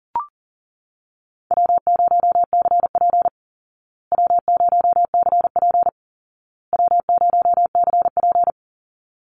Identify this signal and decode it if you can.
Morse code, heard as a keyed tone, about 40 wpm, 700 Hz. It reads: W0CP W0CP W0CP
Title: Easy Call Signs Repeated 3x - 40wpm